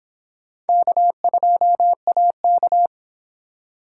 Morse code, heard as a keyed tone, about 26 wpm, 700 Hz. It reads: K2AK